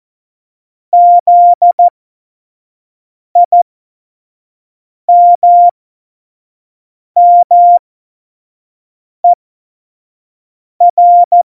Morse code, heard as a keyed tone, about 14 wpm, spaced as for 5 wpm, 700 Hz. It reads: ZIMMER
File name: Morsekurs / Witze / Witz 29. September 2020, 25 F70